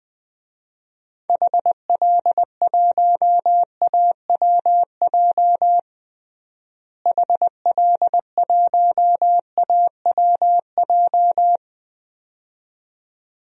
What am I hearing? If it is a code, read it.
Morse code, heard as a keyed tone, about 20 wpm, 700 Hz. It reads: HL1AWJ HL1AWJ
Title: DXCC-100 Call Signs - 6 Characters - Repeated 2x - 20wpm